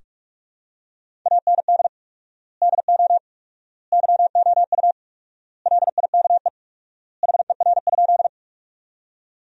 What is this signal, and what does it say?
Morse code, heard as a keyed tone, about 48 wpm, 700 Hz. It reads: AND DO YOU LIKE HER?
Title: 4x Word-Spacing - Sentences from Top 100 Words - 45wpm